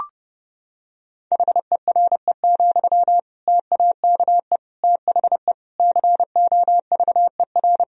Morse code, heard as a keyed tone, about 32 wpm, 700 Hz. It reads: HERE, TAKE THE COVER
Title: Sentences from Top 1000 Words - Review Part 1 - 30wpm